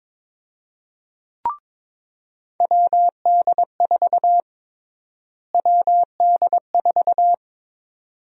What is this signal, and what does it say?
Morse code, heard as a keyed tone, about 22 wpm, 700 Hz. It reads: WD4 WD4
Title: USA Call Sign Prefixes - Repeated 2x - 22wpm